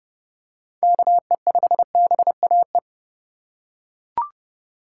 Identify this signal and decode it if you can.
Morse code, heard as a keyed tone, about 32 wpm, 700 Hz. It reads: KE5BAE
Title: USA Call Signs - 6 Characters - 3x Character Spacing - 30wpm